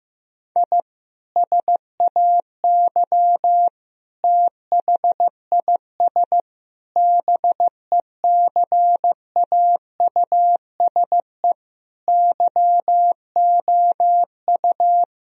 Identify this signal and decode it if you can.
Morse code, heard as a keyed tone, about 16 wpm, 700 Hz. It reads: I SAY THIS BECAUSE YOU
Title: Sentences from Top 1000 Words - Review Part 3 - 15wpm